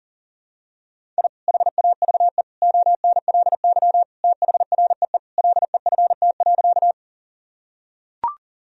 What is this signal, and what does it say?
Morse code, heard as a keyed tone, about 40 wpm, 700 Hz. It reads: I HAVE ONLY THREE LEFT.
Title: Sentences from Top 1000 Words - Review Part 1 - 40wpm